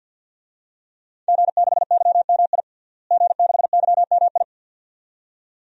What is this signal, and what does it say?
Morse code, heard as a keyed tone, about 52 wpm, 700 Hz. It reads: G6YMI G6YMI
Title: DXCC-100 Call Signs - 5 Characters - Repeated 2x - 50wpm